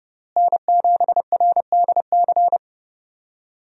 Morse code, heard as a keyed tone, about 30 wpm, 700 Hz. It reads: N7RDC
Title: USA Call Signs - 5 Characters - Repeated 2x - 30wpm